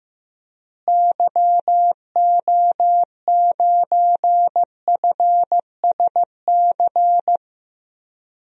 Morse code, heard as a keyed tone, about 15 wpm, 700 Hz. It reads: YO9FSC
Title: DXCC-100 Call Signs - 6 Characters - Repeated 2x - 15wpm